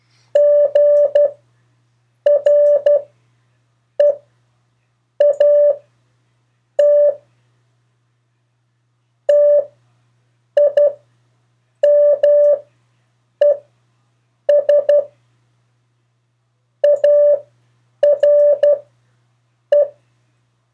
Morse code, heard as a keyed tone, about 12 wpm, spaced as for 7 wpm, 580 Hz. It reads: GREAT TIMES ARE
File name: 4of4-sentences-mistakes-test